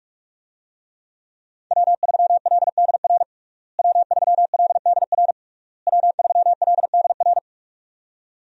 Morse code, heard as a keyed tone, about 46 wpm, 700 Hz. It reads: W3LDR W3LDR W3LDR
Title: USA Call Signs - 5 Characters - Repeated 3x - 45wpm